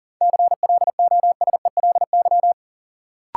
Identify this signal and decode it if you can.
Morse code, heard as a keyed tone, about 40 wpm, 700 Hz. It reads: CLOSELY
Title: Sentences from Top 1400 Words - New Word Drill - 40wpm